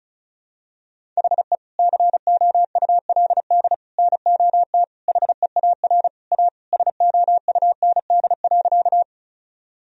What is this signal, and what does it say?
Morse code, heard as a keyed tone, about 34 wpm, 700 Hz. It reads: HE COULD NOT HEAR A SOUND.
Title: Sentences from Top 1000 Words - Review Part 1 - 35wpm